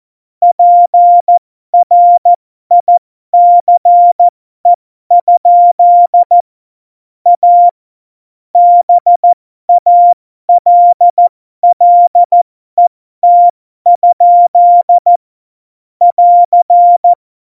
Morse code, heard as a keyed tone, about 14 wpm, spaced as for 12 wpm, 700 Hz. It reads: PRICE? A BALLET? <AR>
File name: Morsekurs / Witze